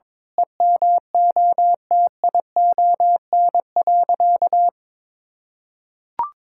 Morse code, heard as a keyed tone, about 22 wpm, 700 Hz. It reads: EMOTION.
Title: Sentences from Top 1600 Words 22wpm